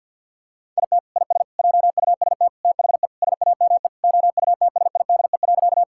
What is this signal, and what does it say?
Morse code, heard as a keyed tone, about 50 wpm, 700 Hz. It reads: IT IS JUST THE SAME OUTSIDE.
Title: Sentences from Top 1000 Words - Review Part 6 - 50wpm